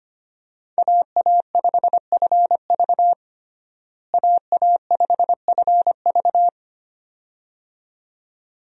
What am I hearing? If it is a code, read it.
Morse code, heard as a keyed tone, about 25 wpm, 700 Hz. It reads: AA5FV AA5FV
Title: USA Call Signs - 5 Characters - Repeated 2x - 25wpm